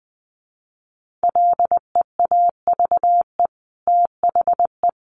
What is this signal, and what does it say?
Morse code, heard as a keyed tone, about 20 wpm, 700 Hz. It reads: LEAVE THE
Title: Sentences from Top 1500 Words 20wpm